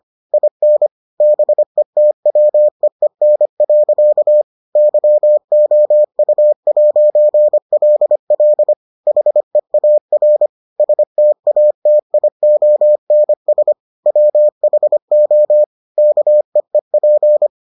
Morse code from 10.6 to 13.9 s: STATIONS

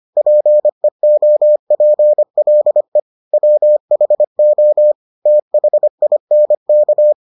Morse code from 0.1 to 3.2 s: PEOPLE

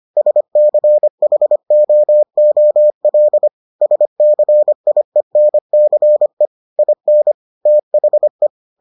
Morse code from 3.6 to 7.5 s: SCIENCE IN

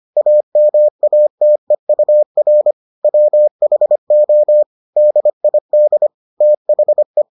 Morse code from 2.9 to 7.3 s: WHO DID THE

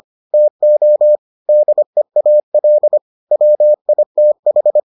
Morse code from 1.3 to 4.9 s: DEAL WITH

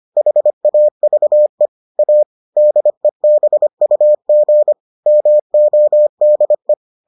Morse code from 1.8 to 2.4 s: A

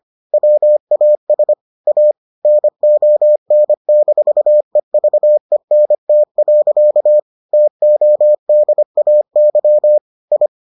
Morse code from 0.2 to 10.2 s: WAS A NON-EVENT. TODAY

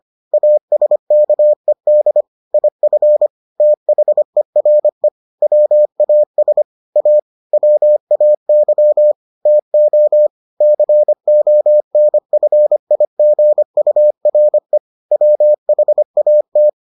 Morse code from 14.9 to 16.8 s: WHAT

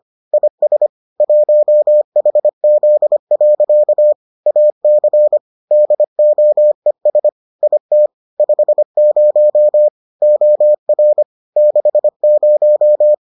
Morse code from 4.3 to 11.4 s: AC DOES IT 50 OR

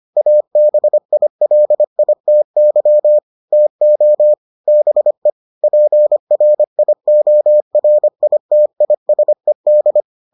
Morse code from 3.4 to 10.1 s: TO BE PRIORITISED

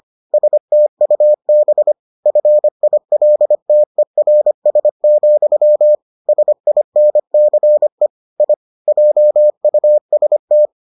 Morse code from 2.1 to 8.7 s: FILTERS, SINCE I

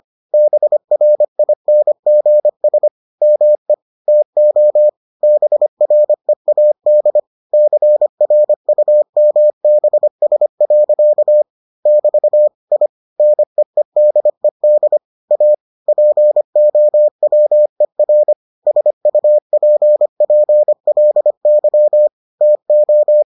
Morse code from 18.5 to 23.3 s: SUPPLY TO

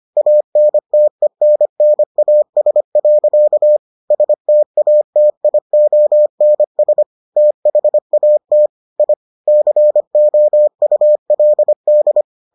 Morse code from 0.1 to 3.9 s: ANTENNAS.

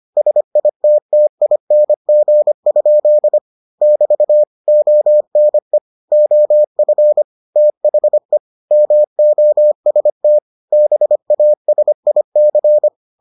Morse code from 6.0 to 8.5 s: OF THE

